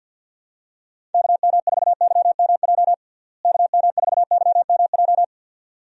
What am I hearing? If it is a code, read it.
Morse code, heard as a keyed tone, about 51 wpm, 700 Hz. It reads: KM4YMJ KM4YMJ
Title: USA Call Signs - 6 Characters - Repeated 2x - 50wpm